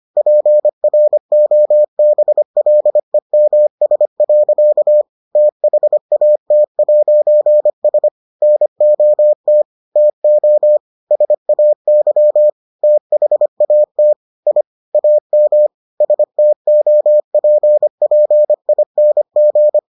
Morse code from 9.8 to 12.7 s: TO SAY

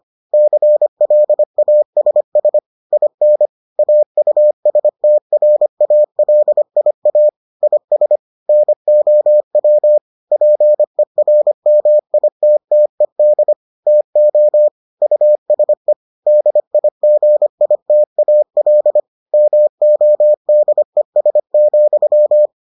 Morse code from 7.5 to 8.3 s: IS

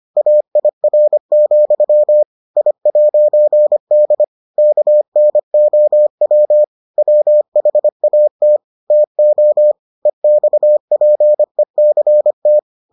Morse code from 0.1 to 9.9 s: AIR, I'D KNOW WHAT TO

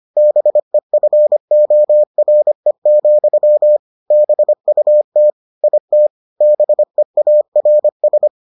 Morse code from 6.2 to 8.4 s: BEARS

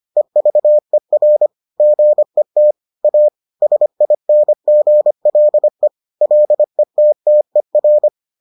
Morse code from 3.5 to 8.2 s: SINGLE LETTER

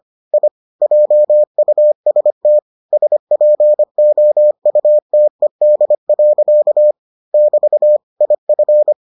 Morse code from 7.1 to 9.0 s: <BT> IF

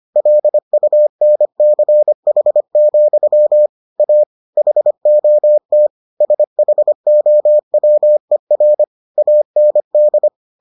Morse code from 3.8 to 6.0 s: A HOT

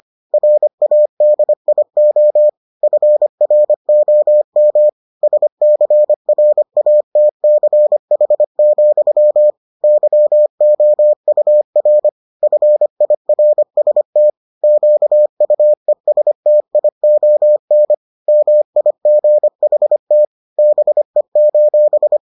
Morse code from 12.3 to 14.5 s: FIRST

